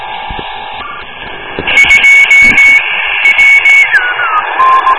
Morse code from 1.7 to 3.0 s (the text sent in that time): 2